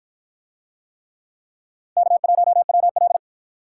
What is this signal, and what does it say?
Morse code, heard as a keyed tone, about 53 wpm, 700 Hz. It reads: K1WL